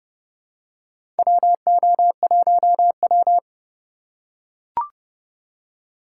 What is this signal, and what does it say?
Morse code, heard as a keyed tone, about 31 wpm, 700 Hz. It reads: WO1W
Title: USA Call Signs - 4 Characters - 6x Character Spacing - 30wpm